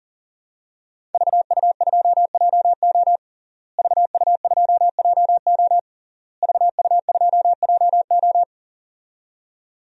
Morse code, heard as a keyed tone, about 40 wpm, 700 Hz. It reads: VU2JO VU2JO VU2JO
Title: DXCC-100 Call Signs - 5 Characters - Repeated 3x - 40wpm